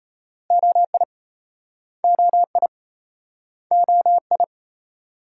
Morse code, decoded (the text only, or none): OI OI OI